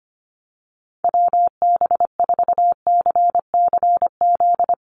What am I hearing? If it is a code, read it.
Morse code, heard as a keyed tone, about 25 wpm, 700 Hz. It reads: WB4CCZ
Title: USA Call Signs Repeated 2x 25wpm